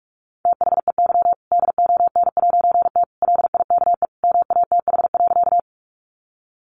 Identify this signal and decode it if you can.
Morse code, heard as a keyed tone, about 44 wpm, 700 Hz. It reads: THEY DON'T LIKE MATH.
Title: Easy Phrases 45wpm